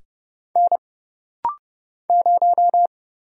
Morse code, heard as a keyed tone, about 30 wpm, 700 Hz. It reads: N 0
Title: Single Letter-Number - Rapid-Fire 30wpm